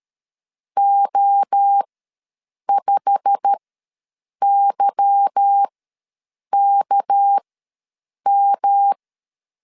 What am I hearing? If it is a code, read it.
Morse code, heard as a keyed tone, about 13 wpm, spaced as for 7 wpm, 800 Hz. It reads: O5YKM
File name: LEZIONI / 8 YJ45 / 8 YJ45 ws 12wpm 3146